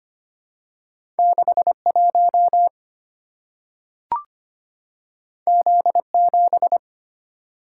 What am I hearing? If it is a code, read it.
Morse code, heard as a keyed tone, about 25 wpm, 700 Hz. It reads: 61 Z7
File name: Two Letter-Number Combinations - 25wpm